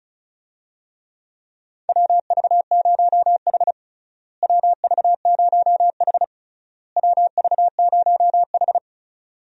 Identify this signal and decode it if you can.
Morse code, heard as a keyed tone, about 36 wpm, 700 Hz. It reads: WV0H WV0H WV0H
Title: USA Call Signs - 4 Characters - Repeated 3x - 35wpm